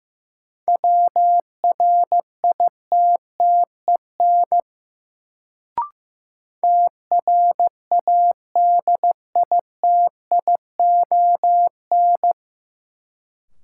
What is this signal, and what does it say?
Morse code, heard as a keyed tone, about 15 wpm, 700 Hz. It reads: WRITTEN TRADITION
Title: Sentences from Top 1500 Words - New Word Drill Alt - 15wpm